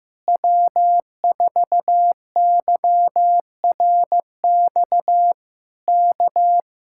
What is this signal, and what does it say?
Morse code, heard as a keyed tone, about 15 wpm, 700 Hz. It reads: W4YRX K
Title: Speed-Racing Common QSO Phrases Encore - 15wpm